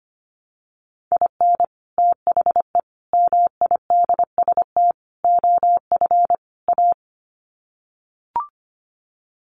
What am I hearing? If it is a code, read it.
Morse code, heard as a keyed tone, about 25 wpm, 700 Hz. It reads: IN THE MIDST OF A
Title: Sets of 5 Words 25wpm